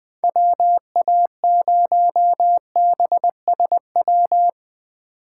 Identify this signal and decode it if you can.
Morse code, heard as a keyed tone, about 20 wpm, 700 Hz. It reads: WA0BSW
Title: USA Call Signs - 6 Characters - 8x Character Spacing - 20wpm